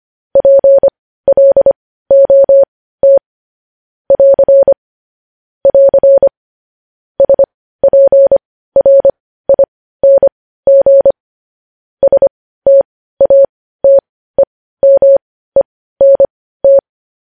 Morse code, heard as a keyed tone, about 25 wpm, spaced as for 15 wpm, 550 Hz. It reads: PLOT <AR> <AR> SPRING STATEMENT
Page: news-headlines-in-morse-code-at-15-wpm